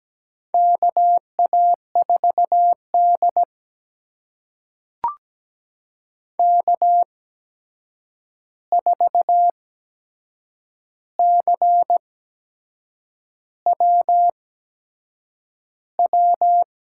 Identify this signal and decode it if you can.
Morse code, heard as a keyed tone, about 17 wpm, spaced as for 7 wpm, 700 Hz. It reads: KA4D K4CWW
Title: USA Call Signs - 8x Character Spacing - 17wpm